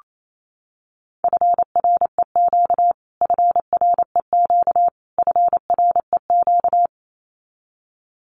Morse code, heard as a keyed tone, about 28 wpm, 700 Hz. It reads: FREQ FREQ FREQ
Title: Abbreviations Repeated 3x - 28wpm